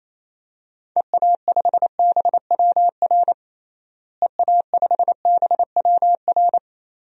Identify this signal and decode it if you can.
Morse code, heard as a keyed tone, about 28 wpm, 700 Hz. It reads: EA5BWR EA5BWR
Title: DXCC-100 Call Signs - 6 Characters - Repeated 3x - 28wpm